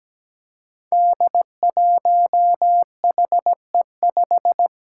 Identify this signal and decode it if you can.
Morse code, heard as a keyed tone, about 17 wpm, 700 Hz. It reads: D1HE5
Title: Five Challenging Letter-Number Combinations 17wpm